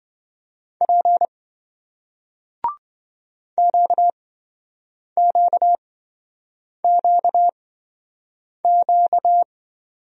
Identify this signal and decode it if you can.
Morse code, heard as a keyed tone, about 26 wpm, spaced as for 7 wpm, 700 Hz. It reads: P QQQQ